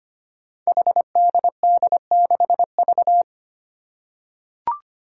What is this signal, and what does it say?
Morse code, heard as a keyed tone, about 25 wpm, 700 Hz. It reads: HDD6V
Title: Five Challenging Letter-Number Combinations 25wpm